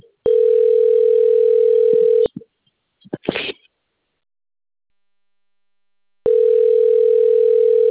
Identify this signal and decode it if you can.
Ringback tone (North America), 440 + 480 Hz.
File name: rg-1600-unknown- - - .61840